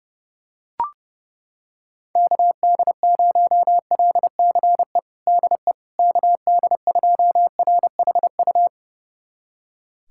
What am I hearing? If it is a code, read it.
Morse code, heard as a keyed tone, about 32 wpm, 700 Hz. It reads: KD0LCE DE KD2RHU
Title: USA Callsign DE Callsign - 6 Characters - 30wpm